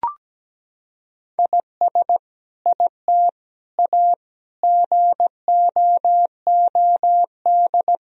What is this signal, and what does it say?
Morse code, heard as a keyed tone, about 17 wpm, 700 Hz. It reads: IS IT A GOOD
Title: Sentences from Top 1000 Words - Review Part 2 - 17wpm